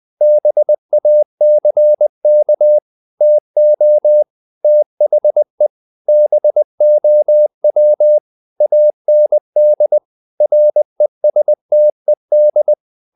Morse code, heard as a keyed tone, about 20 wpm, 600 Hz. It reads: BACK TO THE BOW AND RESTED
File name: the old man and the sea / the old man and the sea part 1 20wpm